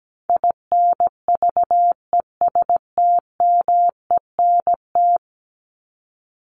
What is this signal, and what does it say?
Morse code, heard as a keyed tone, about 17 wpm, 700 Hz. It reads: INVESTMENT
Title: Top 501-750 Words 17wpm